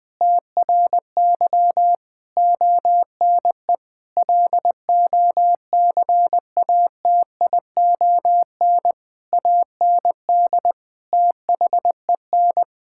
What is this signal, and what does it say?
Morse code, heard as a keyed tone, about 20 wpm, 700 Hz. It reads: TRY ONE LOCATION AND THEN
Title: Sentences from Top 1000 Words - Review Part 6 - 20wpm